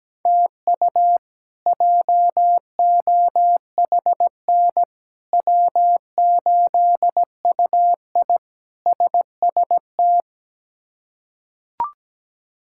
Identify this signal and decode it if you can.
Morse code, heard as a keyed tone, about 17 wpm, 700 Hz. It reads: TU JOHN W8UI SST
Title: K1USN SST QSO Practice - 17wpm